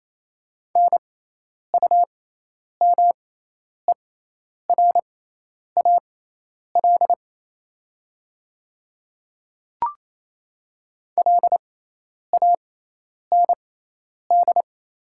Morse code, heard as a keyed tone, about 28 wpm, spaced as for 10 wpm, 700 Hz. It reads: NUMERAL LAND
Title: 6x Character Spacing - Top 500 Words - 28wpm